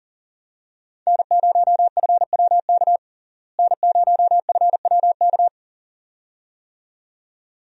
Morse code, heard as a keyed tone, about 40 wpm, 700 Hz. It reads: N0FWK N0FWK
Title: USA Call Signs - 5 Characters - Repeated 2x - 40wpm